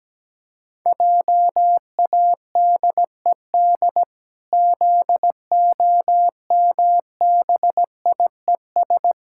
Morse code, heard as a keyed tone, about 17 wpm, 700 Hz. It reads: JADED ZOMBIES
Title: Pangram Challenge v2 - 17wpm